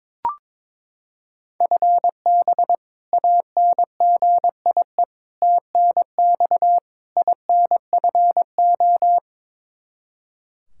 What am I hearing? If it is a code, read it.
Morse code, heard as a keyed tone, about 22 wpm, 700 Hz. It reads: FB ANGIE TNX INFO
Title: K1USN SST QSO Practice 22wpm